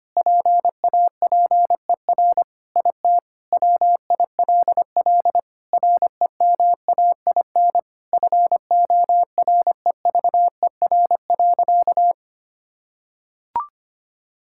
Morse code, heard as a keyed tone, about 25 wpm, 700 Hz. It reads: PAPER IT WILL REMAIN FOREVER.
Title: Sentences from Top 1600 Words 25wpm